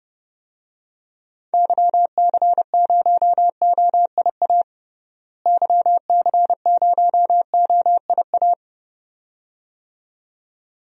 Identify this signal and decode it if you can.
Morse code, heard as a keyed tone, about 32 wpm, 700 Hz. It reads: YC0OIA YC0OIA
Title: DXCC-100 Call Signs - 6 Characters - Repeated 2x - 30wpm